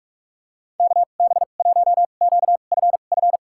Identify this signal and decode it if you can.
Morse code, heard as a keyed tone, about 46 wpm, spaced as for 33 wpm, 700 Hz. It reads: KB1QFF
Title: USA Call Signs - 6 Characters - 2x Character Spacing - 45wpm